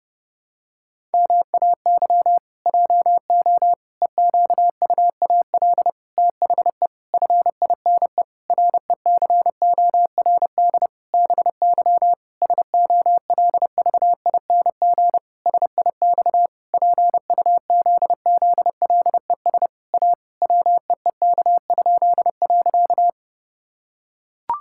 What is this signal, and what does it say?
Morse code, heard as a keyed tone, about 30 wpm, 700 Hz. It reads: MAY JO EQUAL THE FINE RECORD BY SOLVING SIX PUZZLES A WEEK?.
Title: Pangram Challenge v2 - 30wpm